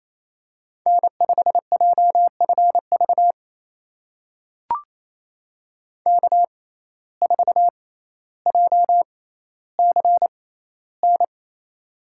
Morse code, heard as a keyed tone, about 28 wpm, spaced as for 14 wpm, 700 Hz. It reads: N5JFV K4JCN
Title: USA Call Signs - 5 Characters - 6x Character Spacing - 28wpm